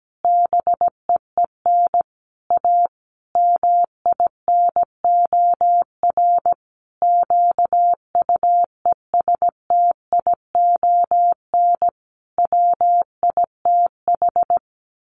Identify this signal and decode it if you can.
Morse code, heard as a keyed tone, about 17 wpm, 700 Hz. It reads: BEEN A MINOR QUESTION WITH
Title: Sentences from Top 1500 Words 17wpm